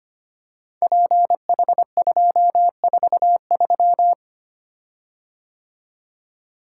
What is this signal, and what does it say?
Morse code, heard as a keyed tone, about 25 wpm, 700 Hz. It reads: PH243
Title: Code Groups - 5 Letter:Numbers - 25wpm